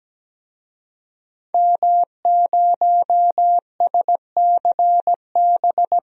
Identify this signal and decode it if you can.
Morse code, heard as a keyed tone, about 17 wpm, 700 Hz. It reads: M0SCB
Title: DXCC-100 Call Signs - 5 Characters - 17wpm